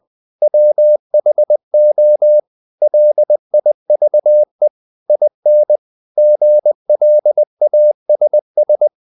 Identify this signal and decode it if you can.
Morse code, heard as a keyed tone, about 20 wpm, 600 Hz. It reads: WHO LIVE IN GLASS
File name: CW / Practice / EnglishSayings20